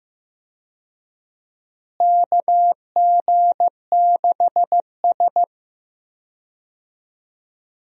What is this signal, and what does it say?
Morse code, heard as a keyed tone, about 15 wpm, 700 Hz. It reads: KG6S
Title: USA Call Signs - 4 Characters - Repeated 3x - 15wpm